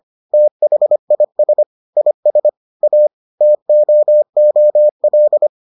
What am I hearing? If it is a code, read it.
Morse code, heard as a keyed tone, about 25 wpm, 600 Hz. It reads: THIS IS A TOOL